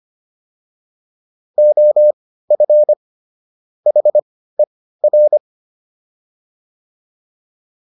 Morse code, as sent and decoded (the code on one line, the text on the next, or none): --- ..-. / .... .-.
OF HER